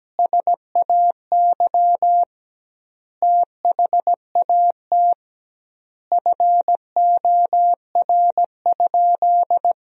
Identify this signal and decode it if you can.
Morse code, heard as a keyed tone, about 17 wpm, 700 Hz. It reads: SAY THAT FOR?
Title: 2x Word-Spacing - Sentences from Top 200 Words - 17wpm